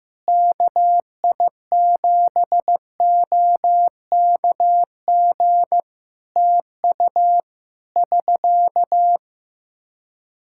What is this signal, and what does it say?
Morse code, heard as a keyed tone, about 15 wpm, 700 Hz. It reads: KI7OKG TU <SK>
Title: K1USN SST QSO Practice 15wpm